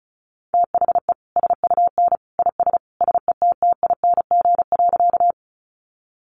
Morse code, heard as a keyed tone, about 34 wpm, 700 Hz. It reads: THE SUN IS SETTING.
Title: Sentences from Top 1300 Words 35wpm